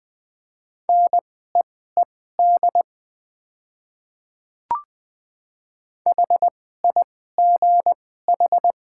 Morse code, heard as a keyed tone, about 20 wpm, spaced as for 14 wpm, 700 Hz. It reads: NEED HIGH
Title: 2x Character Spacing - Top 200 Words - 20wpm